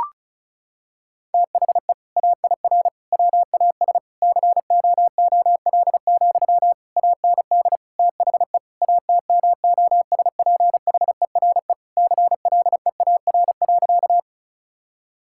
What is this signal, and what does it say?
Morse code, heard as a keyed tone, about 36 wpm, 700 Hz. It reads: THE AIR WAS COOL, AND THE ATMOSPHERE CLEAR.
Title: Sentences from Top 1600 Words 35wpm